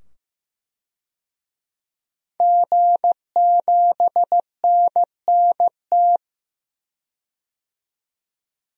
Morse code, heard as a keyed tone, about 15 wpm, 700 Hz. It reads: G7NNT
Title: DXCC-100 Call Signs - 5 Characters - 15wpm